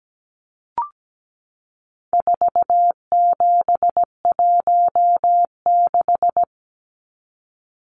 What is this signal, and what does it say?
Morse code, heard as a keyed tone, about 17 wpm, 700 Hz. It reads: 4716
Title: Numbers - 4-Digit 17wpm